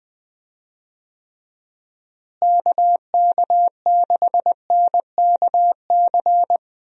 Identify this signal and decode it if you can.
Morse code, heard as a keyed tone, about 19 wpm, 700 Hz. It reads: KK6NKC
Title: USA Call Signs - 6 Characters - Repeated 3x - 20wpm